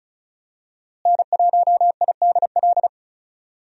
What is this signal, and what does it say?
Morse code, heard as a keyed tone, about 34 wpm, 700 Hz. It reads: N1IDL
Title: USA Call Signs - 5 Characters - Repeated 2x - 35wpm